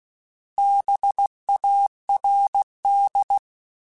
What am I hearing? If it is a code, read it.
Morse code, heard as a keyed tone, about 16 wpm, 800 Hz. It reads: BARD
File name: Alphabetical / Alphabetical mp3s / Bs @15